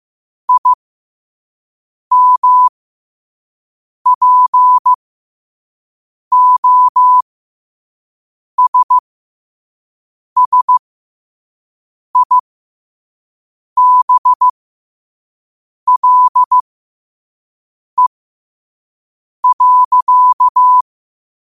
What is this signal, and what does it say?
Morse code, heard as a keyed tone, about 15 wpm, spaced as for 5 wpm, 1000 Hz. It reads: IMPOSSIBLE.
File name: qotdfiles / 5wpm